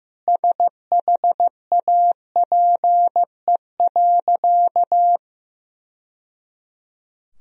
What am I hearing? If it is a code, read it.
Morse code, heard as a keyed tone, about 15 wpm, 700 Hz. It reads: SHAPE.
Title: Sentences from Top 1000 Words - Review Part 3 - 15wpm